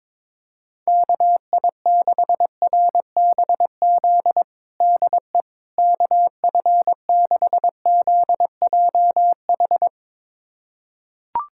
Morse code, heard as a keyed tone, about 22 wpm, 700 Hz. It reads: KI6RBZ DE KF6ZJH
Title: USA Callsign DE Callsign - 6 Characters - 22wpm